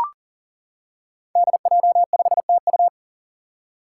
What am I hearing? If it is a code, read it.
Morse code, heard as a keyed tone, about 41 wpm, 700 Hz. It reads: DJ5TU
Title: Five Challenging Letter-Number Combinations 40wpm